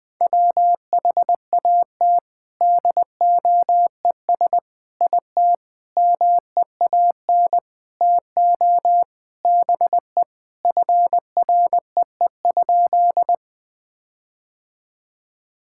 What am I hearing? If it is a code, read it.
Morse code, heard as a keyed tone, about 21 wpm, 700 Hz. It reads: WHAT DOES IT MEAN TO BE FREE?
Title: Sentences with 6-Word Phrases - 20wpm